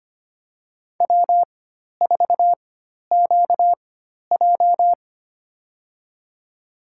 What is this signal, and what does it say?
Morse code, heard as a keyed tone, about 25 wpm, spaced as for 12 wpm, 700 Hz. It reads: W4QJ
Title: USA Call Signs - 4x Character Spacing - 25wpm